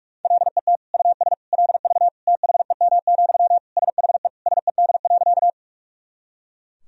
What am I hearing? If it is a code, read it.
Morse code, heard as a keyed tone, about 46 wpm, 700 Hz. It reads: LET US LV THEM, SHE SED.
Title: Sentences from Top 400 Words and Abbreviations 45wpm